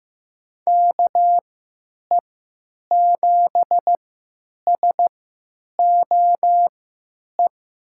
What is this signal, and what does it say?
Morse code, heard as a keyed tone, about 15 wpm, spaced as for 9 wpm, 700 Hz. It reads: KE7SOE
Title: USA Call Signs - 6 Characters - 3x Character Spacing - 15wpm